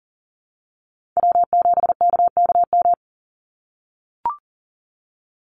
Morse code, heard as a keyed tone, about 38 wpm, 700 Hz. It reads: W7KKM
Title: NA SOTA Chaser Callsigns 40wpm